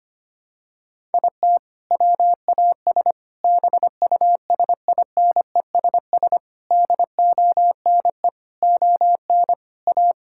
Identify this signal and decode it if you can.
Morse code, heard as a keyed tone, about 25 wpm, 700 Hz. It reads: IT WAS BUSINESS DONE ON A
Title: Sentences from Top 1000 Words - Review Part 5 - 25wpm